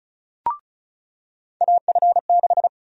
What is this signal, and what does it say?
Morse code, heard as a keyed tone, about 36 wpm, 700 Hz. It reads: AF6
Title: USA Call Sign Prefixes - 35wpm